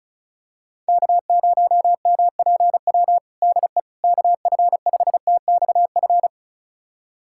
Morse code, heard as a keyed tone, about 34 wpm, 700 Hz. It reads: K0MPW DE KF5TXF
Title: USA Callsign DE Callsign 35wpm